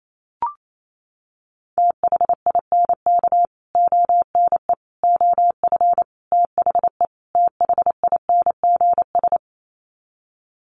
Morse code, heard as a keyed tone, about 28 wpm, 700 Hz. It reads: THINK ONE OF THE THINGS
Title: Sets of 5 Words Encore2 28wpm